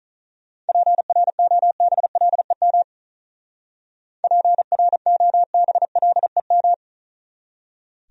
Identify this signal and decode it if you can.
Morse code, heard as a keyed tone, about 44 wpm, 700 Hz. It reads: PROBLEM PROBLEM